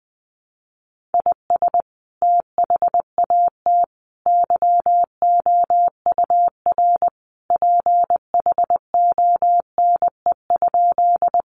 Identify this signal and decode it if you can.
Morse code, heard as a keyed tone, about 20 wpm, 700 Hz. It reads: IS THAT YOUR PHONE?
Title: Sentences from Top 1100 Words 20wpm